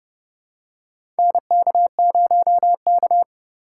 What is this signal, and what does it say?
Morse code, heard as a keyed tone, about 30 wpm, 700 Hz. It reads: NK0K
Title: USA Call Signs - 4 Characters - 2x Character Spacing - 30wpm